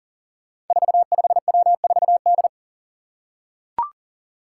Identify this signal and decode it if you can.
Morse code, heard as a keyed tone, about 40 wpm, 700 Hz. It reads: 45W4D
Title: Five Challenging Letter-Number Combinations 40wpm